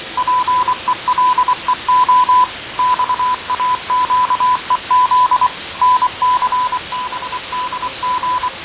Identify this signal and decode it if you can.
Morse code, heard as a keyed tone, about 24 wpm, 1000 Hz. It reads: PELEO XAQEZ NCBDG
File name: fav22-6824usb-20251029-1420z-voice-test-win10